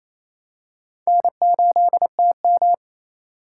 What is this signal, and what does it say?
Morse code, heard as a keyed tone, about 28 wpm, 700 Hz. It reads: N8TM
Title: USA Call Signs - 4 Characters - 3x Character Spacing - 28wpm